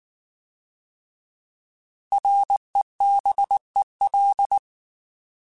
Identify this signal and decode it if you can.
Morse code, heard as a keyed tone, about 19 wpm, 800 Hz. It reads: REBEL
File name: Alphabetical / Alphabetical mp3s / Rs @18